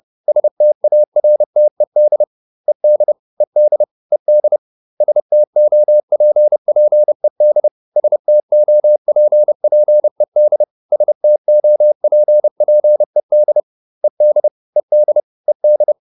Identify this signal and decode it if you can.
Morse code, heard as a keyed tone, about 30 wpm, 600 Hz. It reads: STARTED ED ED ED STOPPED STOPPED STOPPED ED ED ED